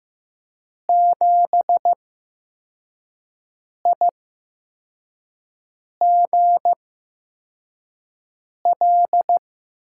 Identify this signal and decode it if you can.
Morse code, heard as a keyed tone, about 15 wpm, spaced as for 4 wpm, 700 Hz. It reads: 7IGL
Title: USA Call Signs - 8x Character Spacing - 15wpm